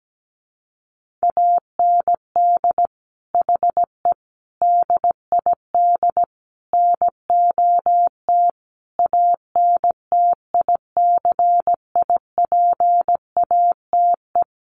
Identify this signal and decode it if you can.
Morse code, heard as a keyed tone, about 17 wpm, 700 Hz. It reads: AND HE DID NOT ANTICIPATE
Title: Sentences from Top 1500 Words 17wpm